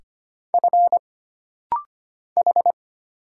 Morse code, heard as a keyed tone, about 25 wpm, 700 Hz. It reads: F H
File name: Single Character (Letter:Number:Punctuation:Prosign) - Rapid-Fire - 25wpm